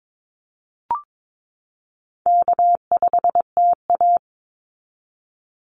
Morse code, heard as a keyed tone, about 22 wpm, 700 Hz. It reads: K5TA
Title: Easy Call Signs Repeated 2x - 22wpm